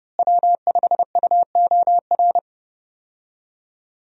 Morse code, heard as a keyed tone, about 32 wpm, 700 Hz. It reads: W5UOR
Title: USA Call Signs - 5 Characters - Repeated 3x - 30wpm